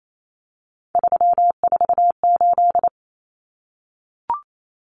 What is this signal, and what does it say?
Morse code, heard as a keyed tone, about 28 wpm, 700 Hz. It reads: 348
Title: Common Cut Numbers - 3-Digit 28wpm